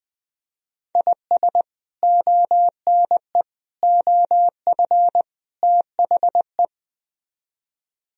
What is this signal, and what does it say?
Morse code, heard as a keyed tone, about 22 wpm, 700 Hz. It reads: IS ONE OF THE
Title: Speed-Racing - Sets of 4 Words 20wpm